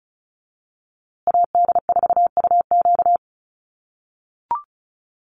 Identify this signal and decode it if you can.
Morse code, heard as a keyed tone, about 34 wpm, 700 Hz. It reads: AD4UQ
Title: USA Call Signs Repeated 3x 35wpm